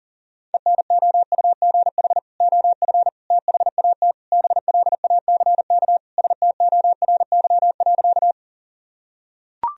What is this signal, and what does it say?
Morse code, heard as a keyed tone, about 37 wpm, 700 Hz. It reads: ENOUGH OF THAT BLACK STORY.
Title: Sentences from Top 1000 Words - Review Part 2 - 40wpm